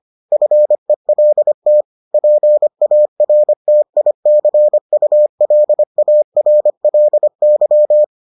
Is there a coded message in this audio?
Yes: Morse code from 0.3 to 8.0 s, heard as a keyed tone, about 25 wpm, 600 Hz. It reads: FELT PARTICULARLY